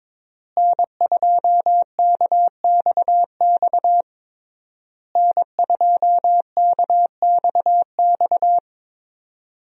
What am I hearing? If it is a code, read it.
Morse code, heard as a keyed tone, about 22 wpm, 700 Hz. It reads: N2KXX N2KXX
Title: USA Call Signs - 5 Characters - Repeated 3x - 22wpm